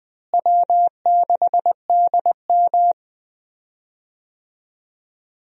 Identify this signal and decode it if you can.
Morse code, heard as a keyed tone, about 21 wpm, 700 Hz. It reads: W6DM
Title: USA Call Signs - 4 Characters - Repeated 3x - 20wpm